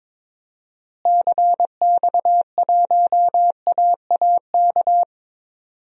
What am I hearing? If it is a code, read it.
Morse code, heard as a keyed tone, about 21 wpm, 700 Hz. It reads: CX1AAK
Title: DXCC-100 Call Signs - 6 Characters - Repeated 2x - 22wpm